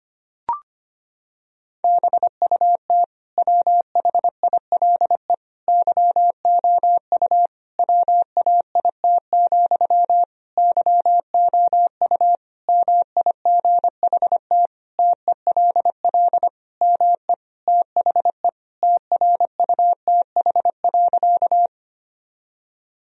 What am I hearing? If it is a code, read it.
Morse code, heard as a keyed tone, about 25 wpm, 700 Hz. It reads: BUT WHILE YOU WAIT, YOU MIGHT TELL ME THE TRUTH.
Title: Sentences from Top 1000 Words - Review Part 1 - 25wpm